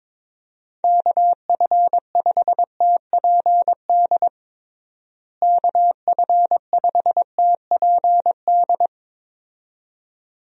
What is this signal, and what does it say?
Morse code, heard as a keyed tone, about 21 wpm, 700 Hz. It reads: KF5TPD KF5TPD
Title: USA Call Signs - 6 Characters - Repeated 2x - 22wpm